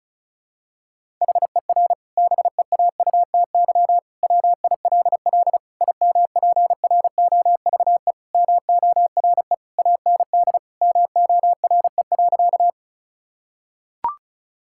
Morse code, heard as a keyed tone, about 36 wpm, 700 Hz. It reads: HER BEAUTY WILL IMPROVE MORE AND MORE.
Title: Sentences from Top 1000 Words - Review Part 5 - 35wpm